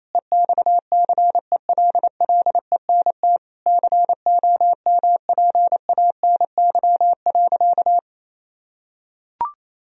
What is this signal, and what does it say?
Morse code, heard as a keyed tone, about 27 wpm, 700 Hz. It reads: EXCELLENT COMPANY.
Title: Sentences from Top 1000 Words - Review Part 3 - 28wpm